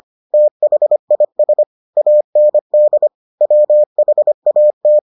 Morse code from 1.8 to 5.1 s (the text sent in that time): AND WHAT